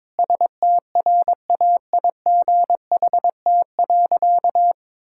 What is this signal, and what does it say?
Morse code, heard as a keyed tone, about 22 wpm, 700 Hz. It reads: STRAIGHT.
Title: Sentences from Top 1000 Words - Review Part 6 - 22wpm